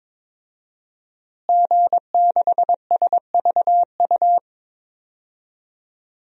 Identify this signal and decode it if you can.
Morse code, heard as a keyed tone, about 22 wpm, 700 Hz. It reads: G6SVU
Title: DXCC-100 Call Signs - 5 Characters - Repeated 3x - 22wpm